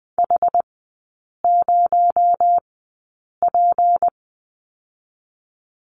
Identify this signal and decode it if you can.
Morse code, heard as a keyed tone, about 20 wpm, 700 Hz. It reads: H 0 P
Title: 218 Lesson 35e - Number 0 - Three Characters 20wpm